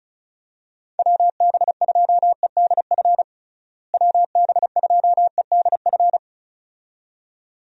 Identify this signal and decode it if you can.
Morse code, heard as a keyed tone, about 34 wpm, 700 Hz. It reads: WB2EDF WB2EDF
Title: USA Call Signs - 6 Characters - Repeated 2x - 35wpm